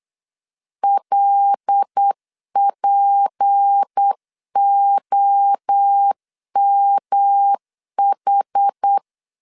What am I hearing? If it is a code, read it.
Morse code, heard as a keyed tone, about 8 wpm, 800 Hz. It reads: LPOMH
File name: LEZIONI / 6 PKX0 / 6 PKX 8wpm 3041